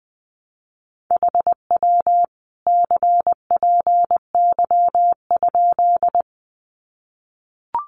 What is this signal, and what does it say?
Morse code, heard as a keyed tone, about 20 wpm, 700 Hz. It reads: HW CPY?
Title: Common QSO Phrases 20wpm